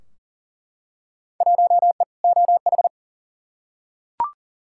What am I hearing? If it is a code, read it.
Morse code, heard as a keyed tone, about 40 wpm, 700 Hz. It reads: 1E OH
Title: Field Day Exchanges - 40wpm